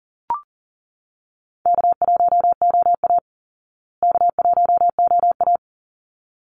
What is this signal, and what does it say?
Morse code, heard as a keyed tone, about 40 wpm, 700 Hz. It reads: K1OA K1OA
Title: Call Signs Repeated 3x - 40wpm